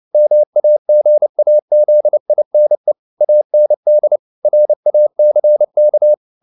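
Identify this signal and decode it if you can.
Morse code, heard as a keyed tone, about 29 wpm, 600 Hz. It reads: MAGAZINE AND RACK
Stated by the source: Morse code at 29 wpm, sending Ken Follett's novel Never in English, from CW med Gnister.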